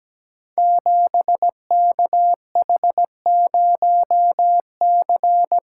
Morse code, heard as a keyed tone, about 17 wpm, 700 Hz. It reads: 7KH0C
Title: Code Groups - 5 Letter:Numbers - 17wpm